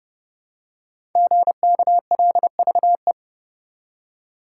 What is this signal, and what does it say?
Morse code, heard as a keyed tone, about 29 wpm, 700 Hz. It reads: GKLVE
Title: Code Groups - 5 Letter:Numbers - 30wpm